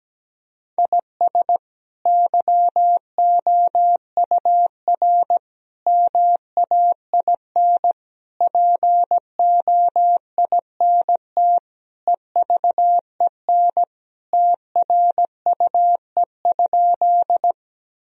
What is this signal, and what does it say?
Morse code, heard as a keyed tone, about 17 wpm, 700 Hz. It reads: IS YOUR MAIN POINT EVEN TRUE?
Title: Sentences from Top 1000 Words - Review Part 2 - 17wpm